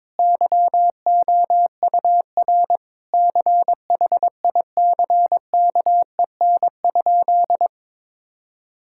Morse code, heard as a keyed tone, about 22 wpm, 700 Hz. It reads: YOUR CHICKEN?
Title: Sentences from Top 1000 Words - Review Part 5 - 22wpm